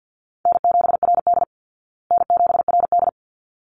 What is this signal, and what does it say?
Morse code, heard as a keyed tone, about 50 wpm, 700 Hz. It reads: N7RD N7RD
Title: Easy Call Signs Repeated 2x - 50wpm